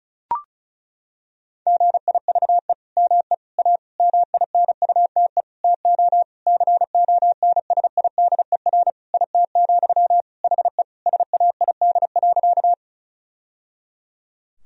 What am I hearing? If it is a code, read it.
Morse code, heard as a keyed tone, about 36 wpm, 700 Hz. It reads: GIVE ME A MINUTE TO CONSIDER IT, HE SAID.
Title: Sentences from Top 1000 Words - Review Part 4 - 35wpm